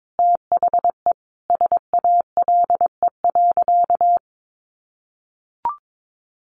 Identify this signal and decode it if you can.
Morse code, heard as a keyed tone, about 22 wpm, 700 Hz. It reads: THE SALE.
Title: Sentences from Top 1400 Words 22wpm